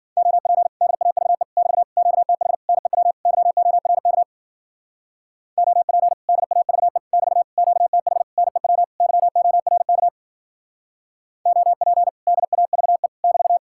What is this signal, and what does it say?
Morse code, heard as a keyed tone, about 52 wpm, 700 Hz. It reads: OP DAVE <BT> QTH NEW YORK OP DAVE <BT> QTH NEW YORK OP DAVE <BT>